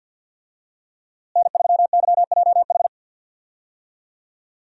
Morse code, heard as a keyed tone, about 51 wpm, 700 Hz. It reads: N3YJH
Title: USA Call Signs - 5 Characters - Repeated 3x - 50wpm